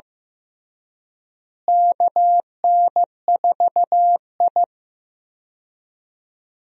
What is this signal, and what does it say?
Morse code, heard as a keyed tone, about 15 wpm, 700 Hz. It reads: KN4I
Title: USA Call Signs - 4 Characters - Repeated 3x - 15wpm